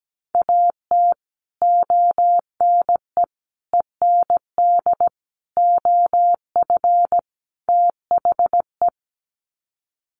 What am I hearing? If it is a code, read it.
Morse code, heard as a keyed tone, about 17 wpm, 700 Hz. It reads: AT ONE END OF THE
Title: Sets of 5 Words Encore 17wpm